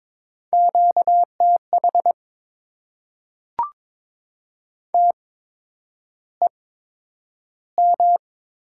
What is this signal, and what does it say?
Morse code, heard as a keyed tone, about 22 wpm, spaced as for 9 wpm, 700 Hz. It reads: QTH TEM
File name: Top 90 QSO Elements - 8x Character Spacing - 22wpm